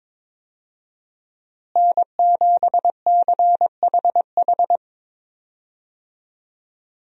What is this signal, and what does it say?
Morse code, heard as a keyed tone, about 22 wpm, 700 Hz. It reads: N7CHH
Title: USA Call Signs - 5 Characters - Repeated 2x - 22wpm